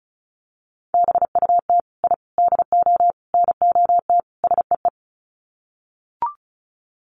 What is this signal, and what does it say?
Morse code, heard as a keyed tone, about 34 wpm, 700 Hz. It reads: BUT I DO NOT SEE
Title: Sets of 5 Words Encore 35wpm